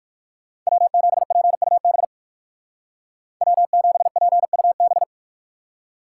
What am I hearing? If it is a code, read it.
Morse code, heard as a keyed tone, about 53 wpm, 700 Hz. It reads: W7PUB W7PUB
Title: Speed Racing - USA 1x3 Call Signs 45wpm